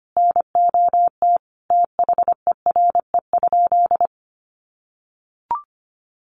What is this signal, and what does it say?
Morse code, heard as a keyed tone, about 25 wpm, 700 Hz. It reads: NOT THERE?
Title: Sentences from Top 600 Words 25wpm